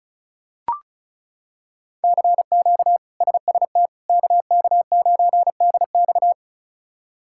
Morse code, heard as a keyed tone, about 36 wpm, 700 Hz. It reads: CQ SST KK9DX
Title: K1USN SST QSO Practice - 35wpm